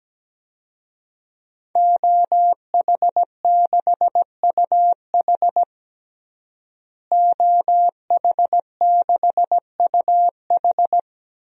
Morse code, heard as a keyed tone, about 17 wpm, 700 Hz. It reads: OH6UH OH6UH
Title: DXCC-100 Call Signs - 5 Characters - Repeated 2x - 17wpm